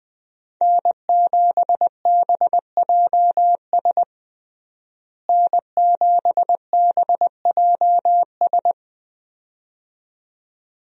Morse code, heard as a keyed tone, about 20 wpm, 700 Hz. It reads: N7BJS N7BJS
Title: USA Call Signs - 5 Characters - Repeated 2x - 20wpm